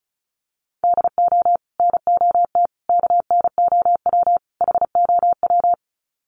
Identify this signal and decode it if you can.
Morse code, heard as a keyed tone, about 34 wpm, 700 Hz. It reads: DO NOT KNOW HOW
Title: Sets of 4 Words 35wpm